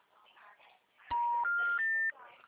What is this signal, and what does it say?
Special information tone, with speech.
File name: out- -2020- - - .316